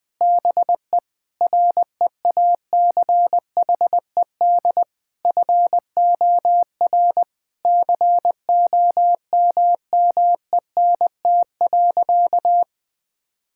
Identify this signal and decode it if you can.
Morse code, heard as a keyed tone, about 21 wpm, 700 Hz. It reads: BE REACHED FOR COMMENT.
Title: Sentences with 6-Word Phrases - Spoken Beginning - 20wpm